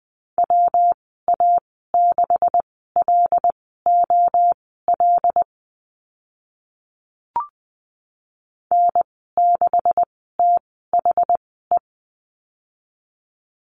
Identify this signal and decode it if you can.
Morse code, heard as a keyed tone, about 20 wpm, spaced as for 14 wpm, 700 Hz. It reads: WA6LOL N6THE
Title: 140 Lesson 23g - Letter f - Call Signs 20wpm